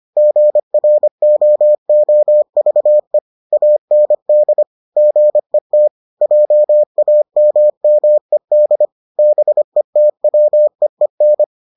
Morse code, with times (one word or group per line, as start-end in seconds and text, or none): 0.2-3.2 GROOVE
3.5-4.6 AND
5.0-5.9 GET
6.2-8.9 JAMMED
9.2-11.4 BETWEEN